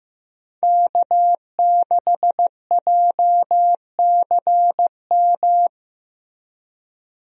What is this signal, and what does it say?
Morse code, heard as a keyed tone, about 15 wpm, 700 Hz. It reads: K6JCM
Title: USA Call Signs - 5 Characters - Repeated 2x - 15wpm